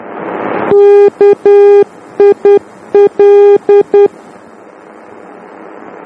Morse code, heard as a keyed tone, about 10 wpm, 400 Hz. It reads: KIL
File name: germany / mx / KIL-D